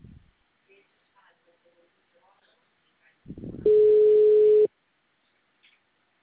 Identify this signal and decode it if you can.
Ringback tone (Europe, 425 Hz), with speech.